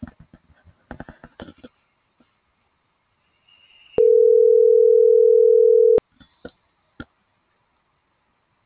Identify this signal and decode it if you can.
Ringback tone (North America), 440 + 480 Hz.